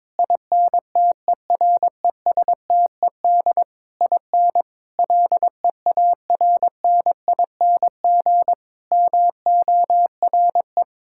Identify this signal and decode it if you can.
Morse code, heard as a keyed tone, about 22 wpm, 700 Hz. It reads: INTERESTED IN LEARNING MORE
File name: practice / 22wpm / Sets of 8 Words - 22wpm